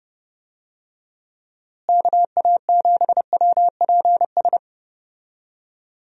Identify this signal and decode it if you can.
Morse code, heard as a keyed tone, about 30 wpm, 700 Hz. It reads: KA7WPS
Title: USA Call Signs - 6 Characters - Repeated 2x - 30wpm